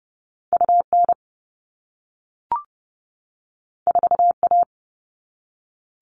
Morse code, heard as a keyed tone, about 30 wpm, 700 Hz. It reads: UN 4A
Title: All Cut Numbers - 2-Digit 30wpm